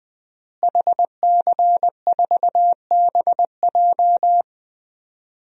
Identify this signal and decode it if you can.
Morse code, heard as a keyed tone, about 20 wpm, 700 Hz. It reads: HC4BJ
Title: DXCC-100 Call Signs - 5 Characters - Repeated 2x - 20wpm